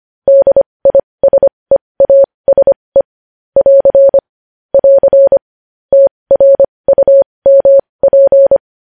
Morse code, heard as a keyed tone, about 25 wpm, spaced as for 20 wpm, 550 Hz. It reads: DISEASE <AR> <AR> TRUMP
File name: podcast / News headlines-1x-25x20